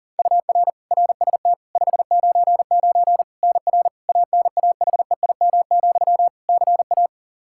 Morse code, heard as a keyed tone, about 37 wpm, 700 Hz. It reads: UR RST 599 NR ANAHEIM, CA
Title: Practice QSO - Exchange 1 - Beginning - 40wpm